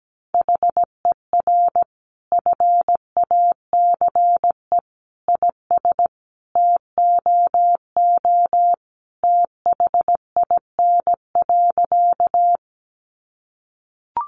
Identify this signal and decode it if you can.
Morse code, heard as a keyed tone, about 17 wpm, 700 Hz. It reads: HER FACE IS TOO THIN.
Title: Sentences from Top 1200 Words 17wpm